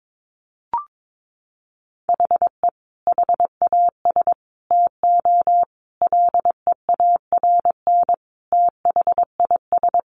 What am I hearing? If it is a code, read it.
Morse code, heard as a keyed tone, about 22 wpm, 700 Hz. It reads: HE HAS TO LEARN THIS
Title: Sentences from Top 1200 Words 22wpm